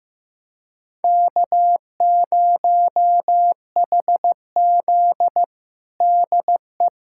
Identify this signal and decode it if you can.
Morse code, heard as a keyed tone, about 16 wpm, 700 Hz. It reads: K0HZ DE
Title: USA Callsign DE Callsign - 4 Characters - 15wpm